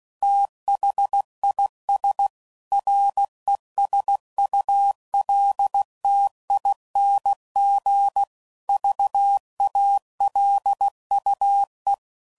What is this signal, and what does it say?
Morse code, heard as a keyed tone, about 16 wpm, 800 Hz. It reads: THIS RESULTING VALUE
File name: CW Code Groups @15WPM / Technical Subjects / XC